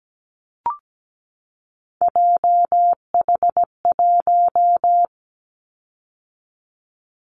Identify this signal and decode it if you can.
Morse code, heard as a keyed tone, about 17 wpm, 700 Hz. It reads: JH1
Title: Callsign Prefixes for Top 25 DXCC - 17wpm